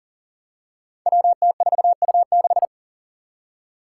Morse code, heard as a keyed tone, about 40 wpm, 700 Hz. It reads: WT4U6
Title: Five Challenging Letter-Number Combinations 40wpm